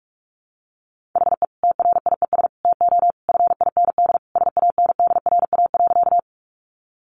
Morse code, heard as a keyed tone, about 44 wpm, 700 Hz. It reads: HE TRIES TO FIND SANDRA.
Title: Easy Phrases 45wpm